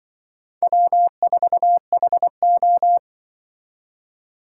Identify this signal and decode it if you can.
Morse code, heard as a keyed tone, about 24 wpm, 700 Hz. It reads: W4HO
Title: Speed Racing - USA 1x2 Call Signs 20wpm